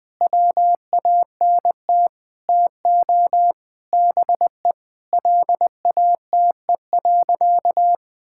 Morse code, heard as a keyed tone, about 20 wpm, 700 Hz. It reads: WANT TO BE LATE.
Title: Sentences with 5-Word Phrases - 20wpm